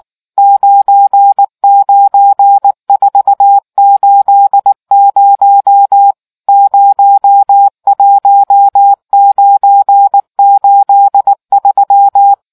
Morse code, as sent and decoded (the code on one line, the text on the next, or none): ----. ----. ....- ---.. ----- / ----- .---- ----. ---.. ...--
99480 01983